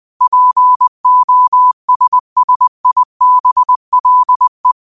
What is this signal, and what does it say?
Morse code, heard as a keyed tone, about 20 wpm, 1000 Hz. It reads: POSSIBLE
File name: qotdfiles / 20wpm